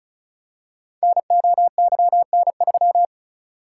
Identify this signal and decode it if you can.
Morse code, heard as a keyed tone, about 36 wpm, 690 Hz. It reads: NOYN3
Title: Code Groups - 5 Letter:Numbers - 35wpm